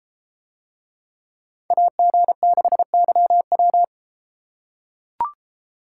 Morse code, heard as a keyed tone, about 33 wpm, 700 Hz. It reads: AG6YW